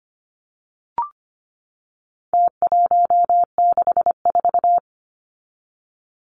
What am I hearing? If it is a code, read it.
Morse code, heard as a keyed tone, about 25 wpm, 700 Hz. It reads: T164
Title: Numbers - Contest Style - 4-Digit 25wpm